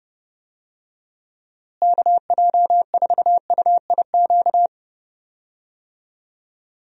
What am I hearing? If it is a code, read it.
Morse code, heard as a keyed tone, about 30 wpm, 700 Hz. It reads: KJ4UIQ
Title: USA Call Signs - 6 Characters - Repeated 2x - 30wpm